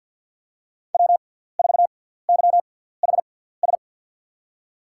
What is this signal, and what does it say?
Morse code, heard as a keyed tone, about 48 wpm, spaced as for 17 wpm, 700 Hz. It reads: W4YHS
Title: USA Call Signs - 6x Character Spacing - 50wpm